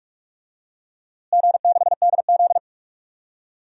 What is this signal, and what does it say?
Morse code, heard as a keyed tone, about 47 wpm, 700 Hz. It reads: G6DZ
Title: DXCC-100 Call Signs - 4 Characters - Repeated 3x - 45wpm